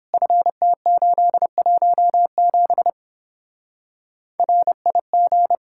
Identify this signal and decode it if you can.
Morse code, heard as a keyed tone, about 31 wpm, 700 Hz. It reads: FT817 RIG